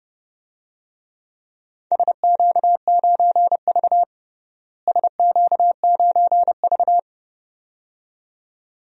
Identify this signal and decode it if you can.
Morse code, heard as a keyed tone, about 32 wpm, 700 Hz. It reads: SQ9V SQ9V
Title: DXCC-100 Call Signs - 4 Characters - Repeated 2x - 30wpm